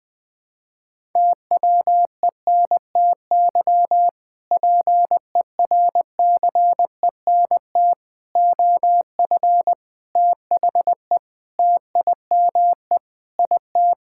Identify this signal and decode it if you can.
Morse code, heard as a keyed tone, about 21 wpm, 700 Hz. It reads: TWENTY PERCENT OF THE TIME IT
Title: Sentences with 2-Word Phrases - Spoken Beginning - 20wpm